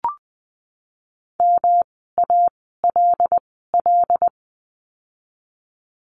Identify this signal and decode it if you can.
Morse code, heard as a keyed tone, about 20 wpm, spaced as for 15 wpm, 700 Hz. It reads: MALL
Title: 108 Lesson 18f - Letter m - Words 20wpm